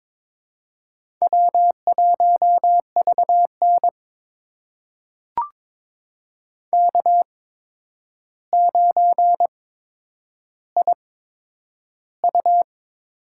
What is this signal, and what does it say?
Morse code, heard as a keyed tone, about 22 wpm, spaced as for 9 wpm, 700 Hz. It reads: W1VN K9IU
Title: USA Call Signs - 4 Characters - 8x Character Spacing - 22wpm